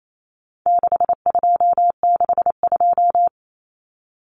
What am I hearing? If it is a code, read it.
Morse code, heard as a keyed tone, about 28 wpm, 700 Hz. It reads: 6262
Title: Numbers - 4-Digit 28wpm